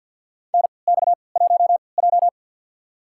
Morse code, heard as a keyed tone, about 52 wpm, spaced as for 29 wpm, 700 Hz. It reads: NX1J
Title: USA Call Signs - 4 Characters - 3x Character Spacing - 50wpm